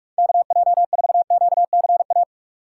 Morse code, heard as a keyed tone, about 46 wpm, 700 Hz. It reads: KJ4QCA
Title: USA Call Signs - 3x Character Spacing - 45wpm